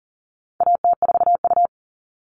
Morse code, heard as a keyed tone, about 40 wpm, 700 Hz. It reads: AT4U
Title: All Cut Numbers - 4-Digit 40wpm